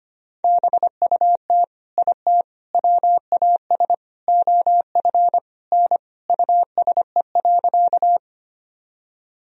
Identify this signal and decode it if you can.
Morse code, heard as a keyed tone, about 25 wpm, 700 Hz. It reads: BUT IT WAS OF N USE.
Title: Sentences from Top 100 Words and Abbreviations 25wpm